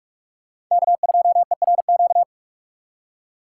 Morse code, heard as a keyed tone, about 46 wpm, 700 Hz. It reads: K2ERQ